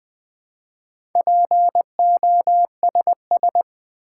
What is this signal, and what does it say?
Morse code, heard as a keyed tone, about 20 wpm, 700 Hz. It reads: POSS
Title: Build A Word - 4 Letters At A Time - 11-Letter Words - 20wpm